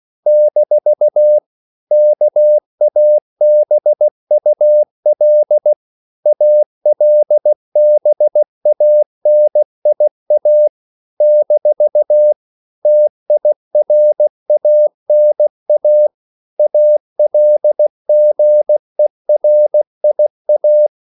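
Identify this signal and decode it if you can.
Morse code, heard as a keyed tone, about 16 wpm, 600 Hz. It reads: - KABUL ALBANIA - TIRANA ALGERIA
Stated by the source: CW practice of national capitals at 16 words per minute, from ON6ZQ.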